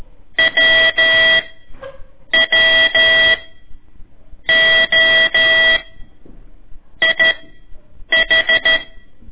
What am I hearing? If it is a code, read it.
Morse code, heard as a keyed tone, about 13 wpm, spaced as for 6 wpm, 1950 Hz. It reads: WWOIH
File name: ik1weg2 / CORSI CW / lez4-1